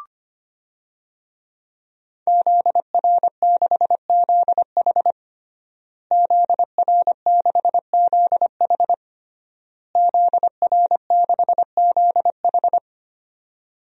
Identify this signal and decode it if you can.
Morse code, heard as a keyed tone, about 25 wpm, 700 Hz. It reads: ZR6ZH ZR6ZH ZR6ZH
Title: DXCC-100 Call Signs - 5 Characters - Repeated 3x - 25wpm